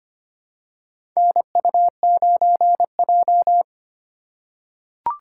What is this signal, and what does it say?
Morse code, heard as a keyed tone, about 25 wpm, 700 Hz. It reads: NU9J
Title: USA Call Signs - 4 Characters - 4x Character Spacing - 25wpm